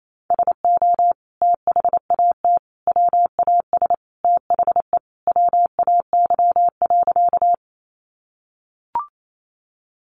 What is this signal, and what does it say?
Morse code, heard as a keyed tone, about 28 wpm, 700 Hz. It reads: SO THAT WAS THE WAY.
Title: Sentences from Top 100 Words 28wpm